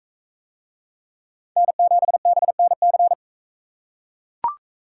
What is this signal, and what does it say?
Morse code, heard as a keyed tone, about 41 wpm, 700 Hz. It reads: N7BNC